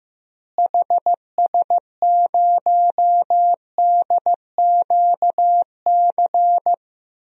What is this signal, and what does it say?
Morse code, heard as a keyed tone, about 15 wpm, 700 Hz. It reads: HS0DQC
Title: DXCC-100 Call Signs - 6 Characters - Repeated 2x - 15wpm